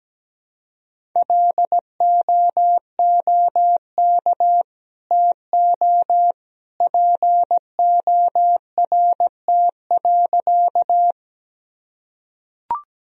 Morse code, heard as a keyed tone, about 17 wpm, 700 Hz. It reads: LOOK TO PORT.
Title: Sentences from Top 1000 Words - Review Part 1 - 17wpm